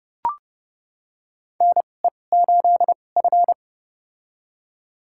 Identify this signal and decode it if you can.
Morse code, heard as a keyed tone, about 30 wpm, spaced as for 22 wpm, 700 Hz. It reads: NE8F
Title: USA Call Signs - 4 Characters - 2x Character Spacing - 30wpm